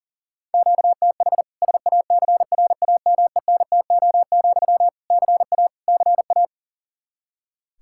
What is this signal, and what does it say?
Morse code, heard as a keyed tone, about 37 wpm, 700 Hz. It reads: QTH SACRAMENTO, CA CA
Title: K1USN SST QSO Practice 40wpm